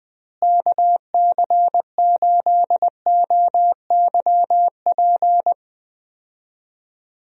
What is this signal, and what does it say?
Morse code, heard as a keyed tone, about 20 wpm, 700 Hz. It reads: KC8OYP
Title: USA Call Signs - 6 Characters - Repeated 2x - 20wpm